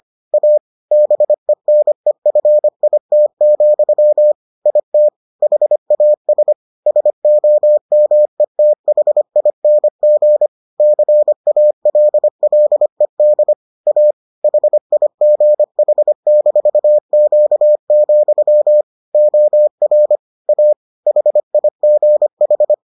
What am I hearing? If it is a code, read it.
Morse code, heard as a keyed tone, about 25 wpm, 600 Hz. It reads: A BENEFIT, IT HAS SOMETHING CALLED A HIGH-Q, OR A HIGH